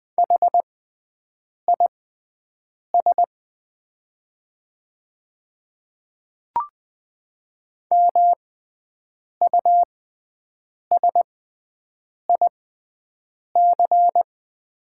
Morse code, heard as a keyed tone, about 21 wpm, spaced as for 7 wpm, 700 Hz. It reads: HIS MUSIC
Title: 6x Character Spacing - Top 300 Words - 20wpm